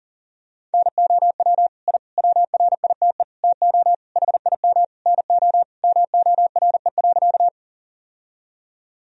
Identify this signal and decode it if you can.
Morse code, heard as a keyed tone, about 40 wpm, 700 Hz. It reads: NOW I WRITE TO HIM NO MORE.
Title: Sentences from Top 1000 Words - Review Part 1 - 40wpm